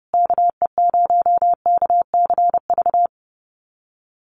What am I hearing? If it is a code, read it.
Morse code, heard as a keyed tone, about 28 wpm, 700 Hz. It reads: KE0KCV
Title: USA Call Signs Repeated 3x 30wpm